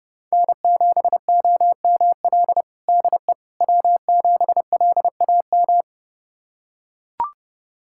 Morse code, heard as a keyed tone, about 32 wpm, 700 Hz. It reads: N7OML DE W7LAM
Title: USA Callsign DE Callsign - 5 Characters - 30wpm